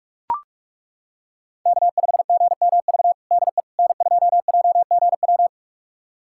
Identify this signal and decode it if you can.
Morse code, heard as a keyed tone, about 41 wpm, 700 Hz. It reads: K5GMV DE N2JGW
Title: USA Callsign DE Callsign - 5 Characters - 45wpm